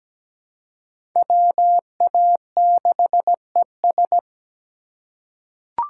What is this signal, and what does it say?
Morse code, heard as a keyed tone, about 17 wpm, 700 Hz. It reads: WA6ES
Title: USA Call Signs - 3x Character Spacing - 17wpm